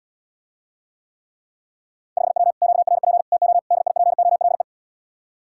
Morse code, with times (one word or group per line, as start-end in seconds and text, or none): 2.2-2.5 5W
2.6-3.2 QRP
3.3-3.6 TO
3.7-4.6 DIPOLE